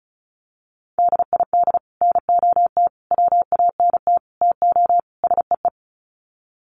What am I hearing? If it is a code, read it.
Morse code, heard as a keyed tone, about 34 wpm, 700 Hz. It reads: DID NOT WANT TO SEE
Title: Sets of 5 Words Encore2 35wpm